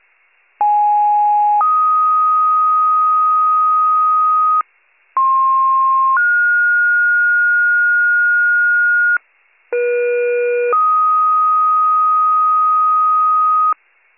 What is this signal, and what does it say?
Two-tone page, heard as a one-second tone, then a three-second tone, about 830 then 1250 Hz, about 1060 then 1430 Hz, about 480 then 1250 Hz. It is heard from 0.6 to 13.7 s.